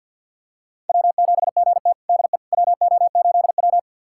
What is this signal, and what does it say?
Morse code, heard as a keyed tone, about 47 wpm, 700 Hz. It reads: W7GT DE WO8W
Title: USA Callsign DE Callsign - 4 Characters - 50wpm